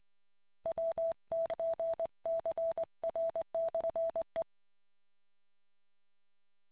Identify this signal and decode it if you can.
Morse code, heard as a keyed tone, about 20 wpm, 660 Hz. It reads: WTGCR/E